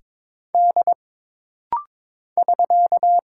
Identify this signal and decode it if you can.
Morse code, heard as a keyed tone, about 23 wpm, 700 Hz. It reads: D <SK>
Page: This rapid-fire timing is geared to push your ICR (Instant Character Recognition) skills.